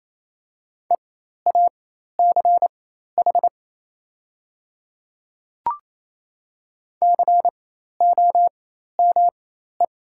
Morse code, heard as a keyed tone, about 28 wpm, spaced as for 13 wpm, 690 Hz. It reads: EACH COME
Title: 4x Character Spacing - Top 100 Words - 28wpm